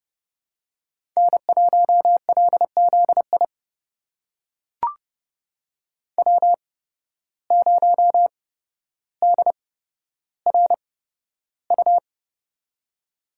Morse code, heard as a keyed tone, about 29 wpm, spaced as for 13 wpm, 700 Hz. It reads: N1LZI W0DRU
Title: USA Call Signs - 8x Character Spacing - 30wpm